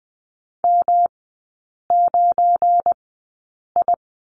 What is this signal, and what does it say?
Morse code, heard as a keyed tone, about 20 wpm, spaced as for 8 wpm, 700 Hz. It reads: M9I
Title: 172 Lesson 28e - Number 9 - Three Characters 20wpm